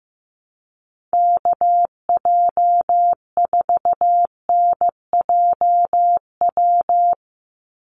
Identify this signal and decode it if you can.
Morse code, heard as a keyed tone, about 15 wpm, 700 Hz. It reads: KJ4NJW
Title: USA Call Signs 15wpm